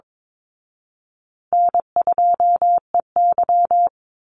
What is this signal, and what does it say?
Morse code, heard as a keyed tone, about 22 wpm, 700 Hz. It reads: N2EY
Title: Call Signs Repeated 2x - 22wpm